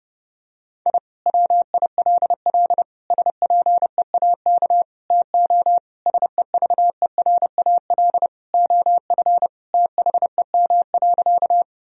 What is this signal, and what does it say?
Morse code, heard as a keyed tone, about 30 wpm, 700 Hz. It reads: I WILL SPEAK TO SEVERAL OF THEM.
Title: Sentences from Top 1000 Words - Review Part 2 - 30wpm